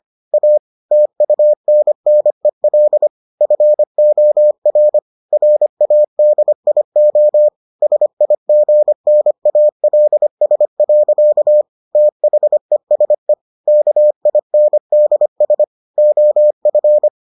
Morse code, heard as a keyed tone, about 25 wpm, 600 Hz. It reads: A TUNNEL FOR RADIO SIGNALS. THESE KINDS OF